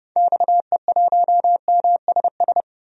Morse code, heard as a keyed tone, about 30 wpm, 700 Hz. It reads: XE1MSS
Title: DXCC-100 Call Signs - 6 Characters - Repeated 3x - 30wpm